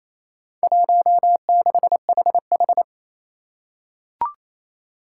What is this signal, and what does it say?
Morse code, heard as a keyed tone, about 28 wpm, 700 Hz. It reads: 16HH
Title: Four Challenging Letter-Number Combinations 28wpm